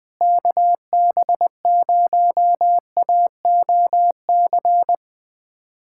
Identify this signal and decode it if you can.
Morse code, heard as a keyed tone, about 21 wpm, 700 Hz. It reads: KB0AOC
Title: USA Callsign DE Callsign - 6 Characters - 20wpm